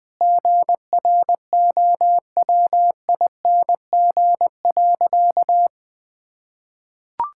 Morse code, heard as a keyed tone, about 20 wpm, 700 Hz. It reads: GROWING.
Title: Sentences with 8-Word Phrases - 20wpm